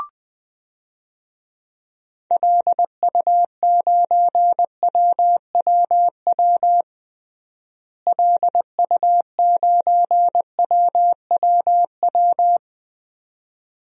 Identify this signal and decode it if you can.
Morse code, heard as a keyed tone, about 20 wpm, 700 Hz. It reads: LU9WWW LU9WWW
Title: DXCC-100 Call Signs - 6 Characters - Repeated 2x - 20wpm